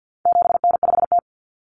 Morse code, heard as a keyed tone, about 48 wpm, 700 Hz. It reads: 7N5T
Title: Common Cut Numbers - 4-Digit 50wpm